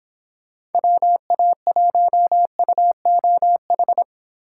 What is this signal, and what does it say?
Morse code, heard as a keyed tone, about 28 wpm, 700 Hz. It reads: WA1UOH